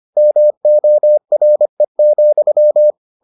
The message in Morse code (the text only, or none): MORE,